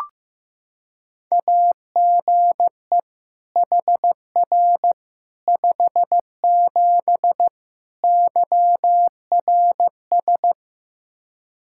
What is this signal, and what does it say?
Morse code, heard as a keyed tone, about 16 wpm, 700 Hz. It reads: AGE HR 57 YRS
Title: Practice QSO - Exchange 3 - More Info - 15wpm